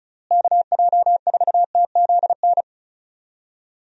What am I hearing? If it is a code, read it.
Morse code, heard as a keyed tone, about 36 wpm, 700 Hz. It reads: KJ4TZN
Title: USA Call Signs - 6 Characters - Repeated 2x - 35wpm